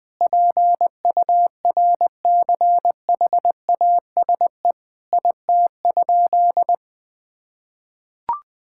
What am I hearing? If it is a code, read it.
Morse code, heard as a keyed tone, about 20 wpm, 700 Hz. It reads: PURCHASE IT?
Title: Sentences from Top 1000 Words - Review Part 5 - 20wpm